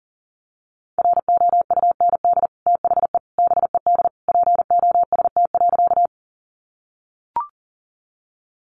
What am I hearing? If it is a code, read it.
Morse code, heard as a keyed tone, about 40 wpm, 700 Hz. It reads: ROUND THE BED POST.
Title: Sentences from Top 1000 Words 40wpm